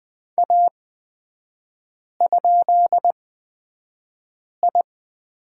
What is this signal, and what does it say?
Morse code, heard as a keyed tone, about 20 wpm, spaced as for 5 wpm, 700 Hz. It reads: A?I
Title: 243 Lesson 39e - Letter z - Three Characters 20wpm